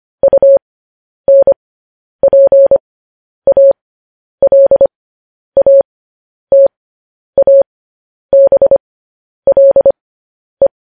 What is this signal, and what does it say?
Morse code, heard as a keyed tone, about 25 wpm, spaced as for 10 wpm, 550 Hz. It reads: UNPALATABLE
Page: news-headlines-in-morse-code-at-10-wpm